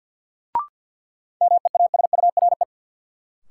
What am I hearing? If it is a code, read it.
Morse code, heard as a keyed tone, about 50 wpm, 700 Hz. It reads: MEASURE
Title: Sentences from Top 1200 Words - New Word Drill Alt - 50wpm